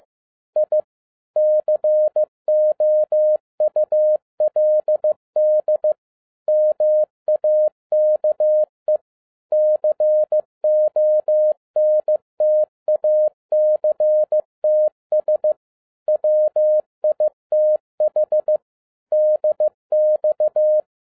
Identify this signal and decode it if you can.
Morse code, heard as a keyed tone, about 15 wpm, 610 Hz. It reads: I COULD MAKE CONTACTS WITH DX
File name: 2011-07-mp3 / n42 EXCITING QRP CW OPERATION 15wpm